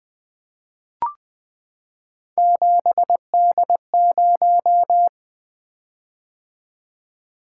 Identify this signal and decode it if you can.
Morse code, heard as a keyed tone, about 20 wpm, 700 Hz. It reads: 7D0
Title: Three Letter-Number Combinations - 20wpm